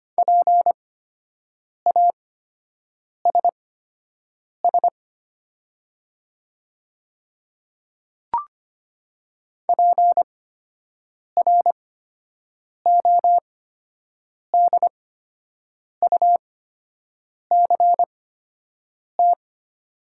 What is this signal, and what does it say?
Morse code, heard as a keyed tone, about 25 wpm, spaced as for 7 wpm, 700 Hz. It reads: PASS PRODUCT
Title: 8x Character Spacing - Top 500 Words - 25wpm